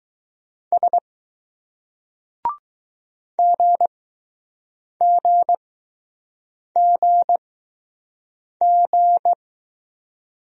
Morse code, heard as a keyed tone, about 20 wpm, spaced as for 6 wpm, 700 Hz. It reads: S GGGG